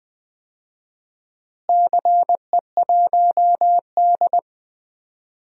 Morse code, heard as a keyed tone, about 20 wpm, 700 Hz. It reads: CE1D
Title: DXCC-100 Call Signs - 4 Characters - Repeated 2x - 20wpm